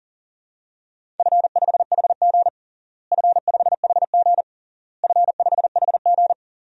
Morse code, heard as a keyed tone, about 37 wpm, 700 Hz. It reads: F5HG F5HG F5HG
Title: DXCC-100 Call Signs - 4 Characters - Repeated 3x - 40wpm